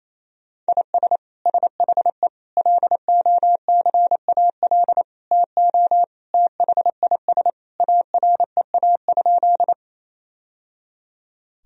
Morse code, heard as a keyed tone, about 28 wpm, 700 Hz. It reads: IS SHE LOCAL TO THIS AREA?
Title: Sentences from Top 1000 Words - Review Part 3 - 28wpm